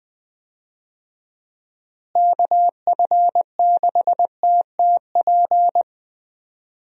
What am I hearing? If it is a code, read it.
Morse code, heard as a keyed tone, about 20 wpm, 700 Hz. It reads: KF6TTP
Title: USA Call Signs - 6 Characters - Repeated 3x - 20wpm